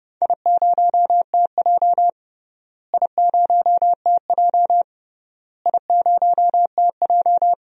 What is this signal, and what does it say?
Morse code, heard as a keyed tone, about 30 wpm, 700 Hz. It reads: I0TJ I0TJ I0TJ
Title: DXCC-100 Call Signs - 4 Characters - Repeated 3x - 30wpm